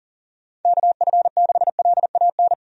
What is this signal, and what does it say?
Morse code, heard as a keyed tone, about 40 wpm, 700 Hz. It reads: KF6LAN
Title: USA Call Signs - 6 Characters - 40wpm